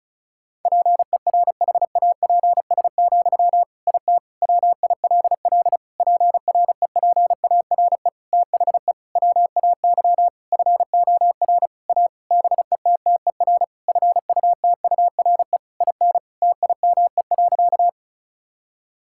Morse code, heard as a keyed tone, about 36 wpm, 700 Hz. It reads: PERHAPS, IT WILL PREPARE THE WAY FOR A BETTER FUTURE IN TIME.
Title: Sentences from Top 1000 Words - Review Part 6 - 35wpm